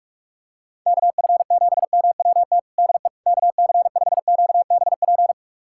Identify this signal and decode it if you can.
Morse code, heard as a keyed tone, about 46 wpm, 700 Hz. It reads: KF7MWT DE KC5QBP
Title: USA Callsign DE Callsign - 6 Characters - 45wpm